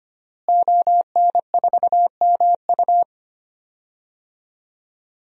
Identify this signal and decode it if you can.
Morse code, heard as a keyed tone, about 25 wpm, 700 Hz. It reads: ON4MU
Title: DXCC-100 Call Signs - 5 Characters - Repeated 3x - 25wpm